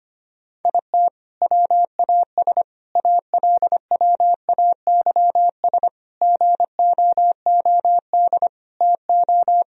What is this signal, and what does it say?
Morse code, heard as a keyed tone, about 25 wpm, 700 Hz. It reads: IT WAS ALWAYS GOOD TO
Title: Sentences from Top 1000 Words - Review Part 4 - 25wpm